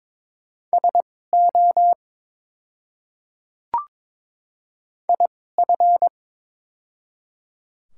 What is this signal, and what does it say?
Morse code, heard as a keyed tone, about 22 wpm, spaced as for 16 wpm, 690 Hz. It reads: SO IF
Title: 2x Character Spacing - Top 100 Words - 22wpm